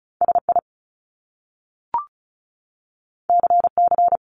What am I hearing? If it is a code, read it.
Morse code, heard as a keyed tone, about 36 wpm, 700 Hz. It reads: SI CC
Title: Top Two Letter Combinations in Words 35wpm